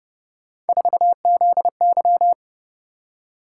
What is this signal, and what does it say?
Morse code, heard as a keyed tone, about 29 wpm, 700 Hz. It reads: 4ZY
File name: Speed-Racing - Three Letter-Number Combinations - 25wpm